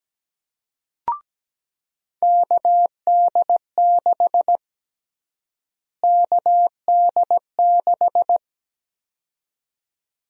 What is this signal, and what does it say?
Morse code, heard as a keyed tone, about 17 wpm, 700 Hz. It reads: KD6 KD6
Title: USA Call Sign Prefixes - Repeated 2x - 17wpm